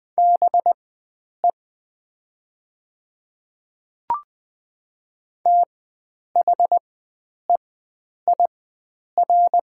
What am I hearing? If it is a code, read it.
Morse code, heard as a keyed tone, about 20 wpm, spaced as for 9 wpm, 700 Hz. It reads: BE THEIR
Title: 4x Character Spacing - Top 100 Words - 20wpm